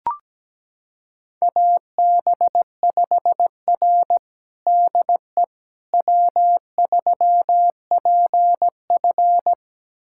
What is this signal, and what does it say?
Morse code, heard as a keyed tone, about 17 wpm, 700 Hz. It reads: AB5R DE W3PF
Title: USA Callsign DE Callsign - 4 Characters - 17wpm